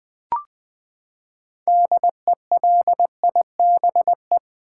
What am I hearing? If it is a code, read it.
Morse code, heard as a keyed tone, about 20 wpm, 700 Hz. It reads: DELIBE
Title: Build A Word - 6 Letters At A Time - 12-Letter Words - 20wpm